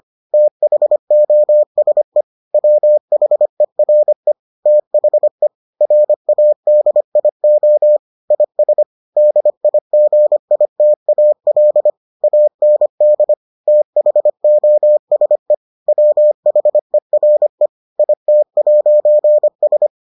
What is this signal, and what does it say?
Morse code, heard as a keyed tone, about 25 wpm, 600 Hz. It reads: THOSE WHERE THE RADIO IS DIGITAL AND THOSE WHERE IT'S